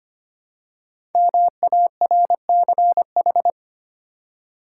Morse code, heard as a keyed tone, about 25 wpm, 700 Hz. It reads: MARCH
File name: 5 Letter Words - First 3 Letters Given - 25wpm